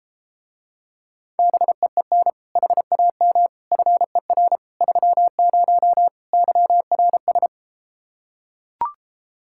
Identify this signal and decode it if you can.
Morse code, heard as a keyed tone, about 33 wpm, 700 Hz. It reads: BEEN HAM FER 30 YRS